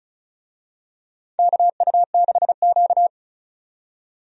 Morse code, heard as a keyed tone, about 36 wpm, 700 Hz. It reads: KU6Q
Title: USA Call Signs - 4 Characters - 35wpm